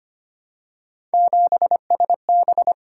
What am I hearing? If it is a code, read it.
Morse code, heard as a keyed tone, about 25 wpm, 700 Hz. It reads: 7SB
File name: Three Letter-Number Combinations - 25wpm